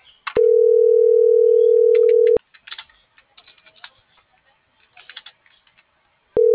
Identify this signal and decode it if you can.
Ringback tone (North America), 440 + 480 Hz.